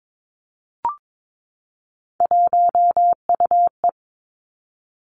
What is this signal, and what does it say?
Morse code, heard as a keyed tone, about 22 wpm, 700 Hz. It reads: 1UE
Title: Three Challenging Letter-Number Combinations 22wpm